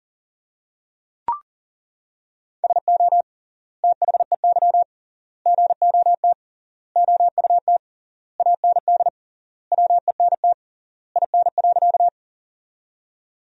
Morse code, heard as a keyed tone, about 40 wpm, 700 Hz. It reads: SO THEY GOT OUT AND WENT IN.
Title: 3x Word-Spacing - Sentences from Top 300 Words - 40wpm